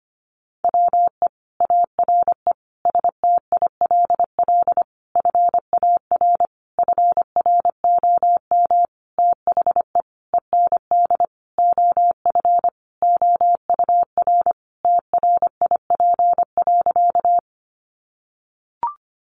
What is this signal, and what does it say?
Morse code, heard as a keyed tone, about 25 wpm, 700 Hz. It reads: WE ARE STILL FAR FROM THE END OF OUR TRIP.
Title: Sentences from Top 1300 Words 25wpm